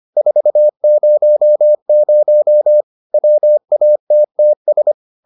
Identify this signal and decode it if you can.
Morse code, heard as a keyed tone, about 25 wpm, 600 Hz. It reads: 400 WATTS